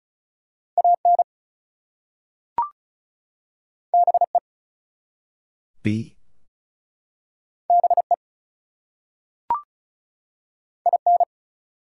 Morse code, heard as a keyed tone, about 36 wpm, 700 Hz. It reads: AN BE BE IN
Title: Two Letter Words - 35wpm